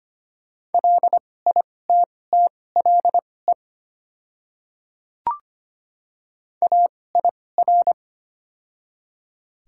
Morse code, heard as a keyed tone, about 25 wpm, spaced as for 18 wpm, 700 Hz. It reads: LITTLE AIR
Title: 2x Character Spacing - Top 500 Words - 25wpm